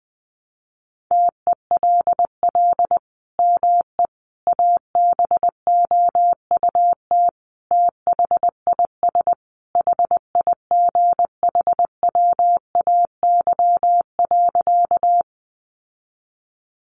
Morse code, heard as a keyed tone, about 20 wpm, 700 Hz. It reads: TELL ME ABOUT THIS HIGHWAY.
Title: Sentences from Top 1400 Words 20wpm